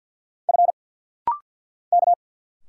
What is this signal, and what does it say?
Morse code, heard as a keyed tone, about 50 wpm, 690 Hz. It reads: F K
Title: Single Letters - Rapid-Fire 50wpm